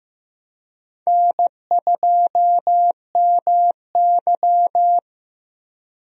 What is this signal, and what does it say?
Morse code, heard as a keyed tone, about 15 wpm, 700 Hz. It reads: N2MY
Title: USA Call Signs - 4 Characters - Repeated 2x - 15wpm